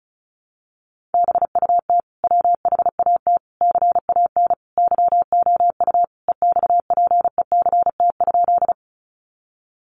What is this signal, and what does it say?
Morse code, heard as a keyed tone, about 34 wpm, 700 Hz. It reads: BUT WHAT CAN YOU EXPECT?
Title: Sentences from Top 1100 Words 35wpm